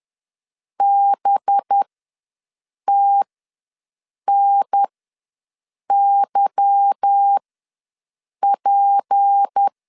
Morse code, heard as a keyed tone, about 10 wpm, spaced as for 6 wpm, 800 Hz. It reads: BTNYP
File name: LEZIONI / 8 YJ45 / 8 Y ws 10wpm 3952